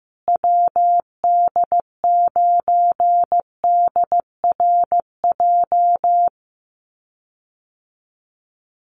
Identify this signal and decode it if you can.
Morse code, heard as a keyed tone, about 15 wpm, 700 Hz. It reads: WD9DRJ
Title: USA Call Signs Repeated 3x 15wpm